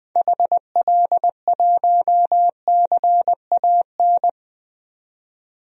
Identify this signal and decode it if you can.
Morse code, heard as a keyed tone, about 20 wpm, 700 Hz. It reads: HL1CAN
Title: DXCC-100 Call Signs - 6 Characters - 20wpm